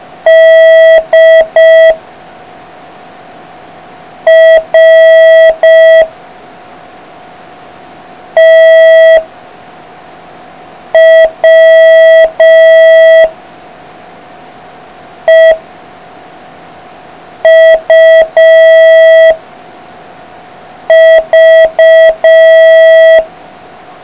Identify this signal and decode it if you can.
Morse code, heard as a keyed tone, about 5 wpm, 660 Hz. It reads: D R T W E U V